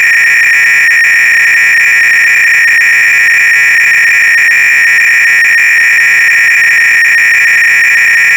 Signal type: 300-baud modem data